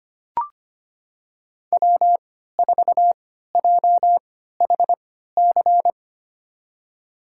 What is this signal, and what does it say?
Morse code, heard as a keyed tone, about 25 wpm, spaced as for 14 wpm, 700 Hz. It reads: W4JHC
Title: USA Call Signs - 5 Characters - 3x Character Spacing - 25wpm